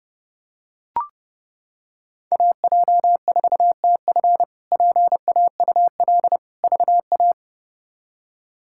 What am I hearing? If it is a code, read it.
Morse code, heard as a keyed tone, about 32 wpm, 700 Hz. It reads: AJ4TF PAUL VA
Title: K1USN SST QSO Practice - 30wpm